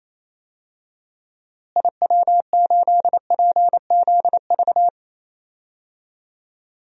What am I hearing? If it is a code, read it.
Morse code, heard as a keyed tone, about 28 wpm, 700 Hz. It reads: IW8PZV
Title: DXCC-100 Call Signs - 6 Characters - Repeated 3x - 28wpm